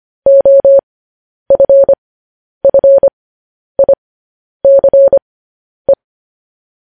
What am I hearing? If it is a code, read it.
Morse code, heard as a keyed tone, about 25 wpm, spaced as for 10 wpm, 550 Hz. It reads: OFFICE
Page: news headlines in morse code at 10 wpm